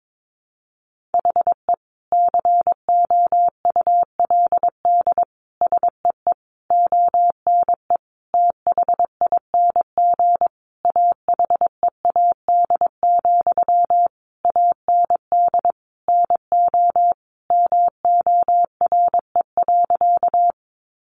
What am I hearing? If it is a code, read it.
Morse code, heard as a keyed tone, about 22 wpm, 700 Hz. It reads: HE COULD SEE ONE THING AHEAD, AND NO MORE.
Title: Sentences from Top 1100 Words 22wpm